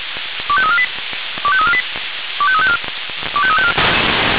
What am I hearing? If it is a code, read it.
Five-tone selective call (ZVEI), heard as quick runs of five tones, tones of about 70 ms. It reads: 25348 25348 25354 25354
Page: SDR/ZVEI